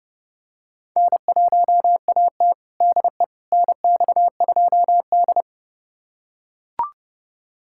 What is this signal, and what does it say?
Morse code, heard as a keyed tone, about 32 wpm, 700 Hz. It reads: N1AT DE NX2D
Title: USA Callsign DE Callsign - 4 Characters - 30wpm